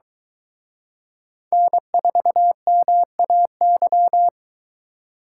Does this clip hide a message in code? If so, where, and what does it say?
Yes: Morse code from 1.5 to 4.3 s, heard as a keyed tone, about 22 wpm, 700 Hz. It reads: N4MAY